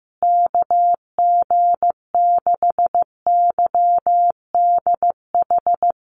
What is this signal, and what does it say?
Morse code, heard as a keyed tone, about 15 wpm, 700 Hz. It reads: KG6YDH
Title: USA Call Signs Repeated 3x 15wpm